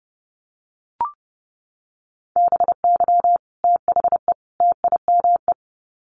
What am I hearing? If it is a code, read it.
Morse code, heard as a keyed tone, about 30 wpm, 700 Hz. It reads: BY THE TIME
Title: Sets of 3 Words 30wpm